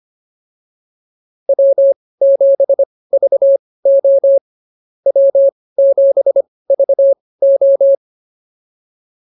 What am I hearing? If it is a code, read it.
Morse code, heard as a keyed tone, about 24 wpm, spaced as for 18 wpm, 550 Hz. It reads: W7VO W7VO